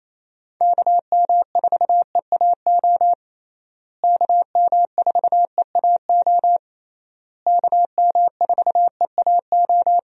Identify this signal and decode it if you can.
Morse code, heard as a keyed tone, about 27 wpm, 700 Hz. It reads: KM4EAO KM4EAO KM4EAO
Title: USA Call Signs - 6 Characters - Repeated 3x - 28wpm